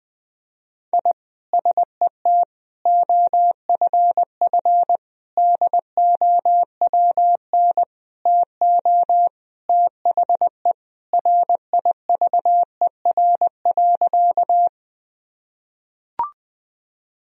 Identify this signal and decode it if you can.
Morse code, heard as a keyed tone, about 20 wpm, 700 Hz. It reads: I SET OFF DOWN TO THE RIVER.
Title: Sentences from Top 1000 Words - Review Part 1 - 20wpm